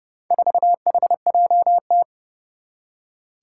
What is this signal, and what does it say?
Morse code, heard as a keyed tone, about 29 wpm, 700 Hz. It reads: 4HJT
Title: Four Challenging Letter-Number Combinations 30wpm